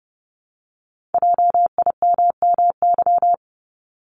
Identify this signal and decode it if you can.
Morse code, heard as a keyed tone, about 30 wpm, 700 Hz. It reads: JIMMY
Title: Multi-letter Phonemes Repeated 3x 30wpm